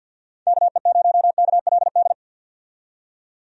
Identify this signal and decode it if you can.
Morse code, heard as a keyed tone, about 49 wpm, 700 Hz. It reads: KE0KLD
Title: USA Call Signs - 6 Characters - 50wpm